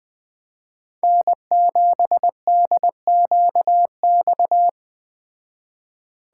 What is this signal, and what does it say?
Morse code, heard as a keyed tone, about 20 wpm, 700 Hz. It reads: N7DQX
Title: Speed Racing - USA 1x3 Call Signs 17wpm